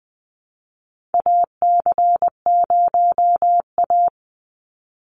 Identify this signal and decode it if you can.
Morse code, heard as a keyed tone, about 20 wpm, 690 Hz. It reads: AC0A
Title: NA SOTA Chaser Callsigns 20wpm